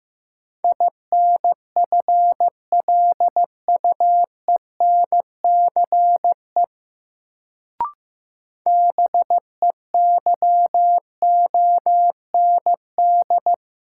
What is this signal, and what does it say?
Morse code, heard as a keyed tone, about 15 wpm, 700 Hz. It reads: INFLUENCE BEYOND
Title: Sentences from Top 1100 Words - New Word Drill Alt - 15wpm